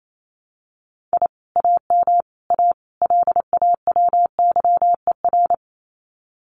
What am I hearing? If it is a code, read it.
Morse code, heard as a keyed tone, about 28 wpm, 700 Hz. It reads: I AM A LAWYER
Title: Common QSO Phrases Encore 28wpm